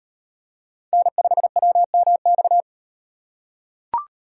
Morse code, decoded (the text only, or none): N5WMX